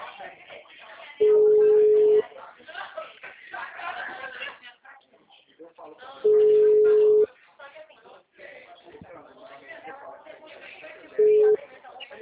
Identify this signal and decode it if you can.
Ringback tone (Europe, 425 Hz), with speech.
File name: out- -2007- - - .90183